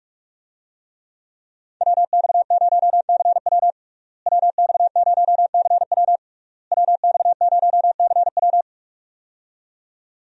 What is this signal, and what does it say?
Morse code, heard as a keyed tone, about 46 wpm, 700 Hz. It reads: WX0CW WX0CW WX0CW
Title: USA Call Signs - 5 Characters - Repeated 3x - 45wpm